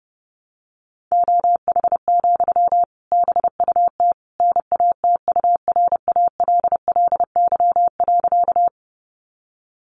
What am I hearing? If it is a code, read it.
Morse code, heard as a keyed tone, about 30 wpm, 700 Hz. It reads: OH, BUT NATURALLY.
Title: Sentences from Top 1300 Words 30wpm